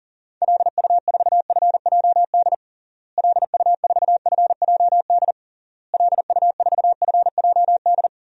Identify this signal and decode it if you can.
Morse code, heard as a keyed tone, about 40 wpm, 700 Hz. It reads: LU4FJD LU4FJD LU4FJD
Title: DXCC-100 Call Signs - 6 Characters - Repeated 3x - 40wpm